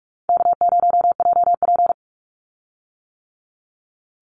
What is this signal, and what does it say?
Morse code, heard as a keyed tone, about 44 wpm, 700 Hz. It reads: K0JP
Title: Easy Call Signs - 45wpm